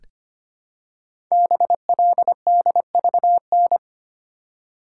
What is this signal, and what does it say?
Morse code, heard as a keyed tone, about 25 wpm, 700 Hz. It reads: BLDVN
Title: Code Groups - 5 Letter:Numbers - 25wpm